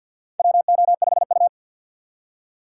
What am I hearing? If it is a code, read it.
Morse code, heard as a keyed tone, about 46 wpm, 700 Hz. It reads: WO5U
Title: USA Call Signs - 4 Characters - 2x Character Spacing - 50wpm